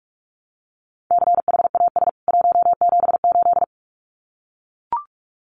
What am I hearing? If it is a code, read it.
Morse code, heard as a keyed tone, about 44 wpm, 700 Hz. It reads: CHAS 178
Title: CWT Exchange 45wpm